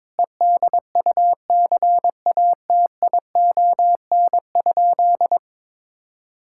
Morse code, heard as a keyed tone, about 22 wpm, 700 Hz. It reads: EDUCATION?
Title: Sentences from Top 1000 Words - Review Part 6 - 22wpm